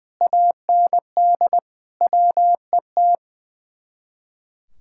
Morse code, heard as a keyed tone, about 20 wpm, 700 Hz. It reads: AND WET
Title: Speed-Racing Common QSO Phrases - 20wpm